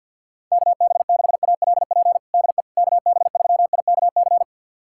Morse code, heard as a keyed tone, about 51 wpm, 700 Hz. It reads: KB6ALP DE KB3IKC
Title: USA Callsign DE Callsign 50wpm